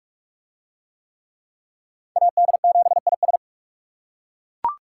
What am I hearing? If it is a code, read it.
Morse code, heard as a keyed tone, about 39 wpm, 700 Hz. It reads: AD7IS